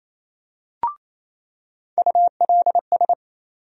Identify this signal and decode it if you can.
Morse code, heard as a keyed tone, about 28 wpm, 700 Hz. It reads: ULS
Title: USA Call Sign Postfixes - 3 Letters 28wpm